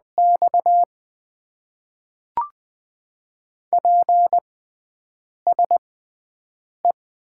Morse code, heard as a keyed tone, about 20 wpm, spaced as for 7 wpm, 700 Hz. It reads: X PSE
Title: Top 90 QSO Elements - 6x Character Spacing - 20wpm